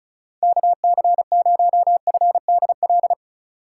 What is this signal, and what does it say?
Morse code, heard as a keyed tone, about 36 wpm, 700 Hz. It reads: KC0FDL
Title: USA Call Signs - 6 Characters - 2x Character Spacing - 35wpm